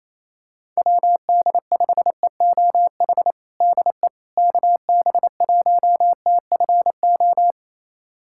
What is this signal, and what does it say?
Morse code, heard as a keyed tone, about 28 wpm, 700 Hz. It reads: WD5EOH DE KB1TFO
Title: USA Callsign DE Callsign - 6 Characters - 28wpm